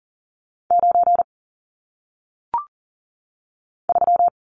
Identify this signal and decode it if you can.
Morse code, heard as a keyed tone, about 39 wpm, 700 Hz. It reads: 9 3
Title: Numbers - 1-Digit 40wpm